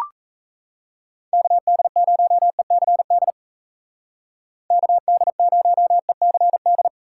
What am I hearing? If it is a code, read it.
Morse code, heard as a keyed tone, about 35 wpm, 700 Hz. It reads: KD0ECD KD0ECD